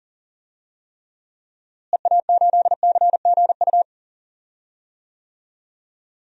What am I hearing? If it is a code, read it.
Morse code, heard as a keyed tone, about 37 wpm, 700 Hz. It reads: EA8CGU
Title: DXCC-100 Call Signs - 6 Characters - Repeated 2x - 40wpm